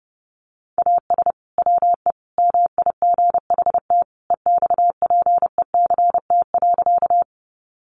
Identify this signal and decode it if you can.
Morse code, heard as a keyed tone, about 28 wpm, 700 Hz. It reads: AS WE MIGHT EXPECT.
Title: Sentences from Top 1100 Words 30wpm